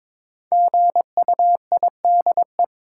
Morse code, heard as a keyed tone, about 21 wpm, 700 Hz. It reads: GUIDE
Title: Sentences from Top 600 Words - New Word Drill - 22wpm